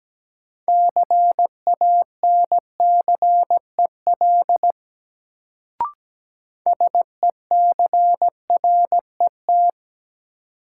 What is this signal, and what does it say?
Morse code, heard as a keyed tone, about 17 wpm, 700 Hz. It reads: CANCEL SECRET
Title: Sentences from Top 1200 Words - New Word Drill - 17wpm